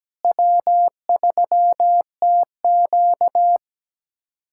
Morse code, heard as a keyed tone, about 17 wpm, 700 Hz. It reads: W3TQ
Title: USA Callsign DE Callsign - 4 Characters - 17wpm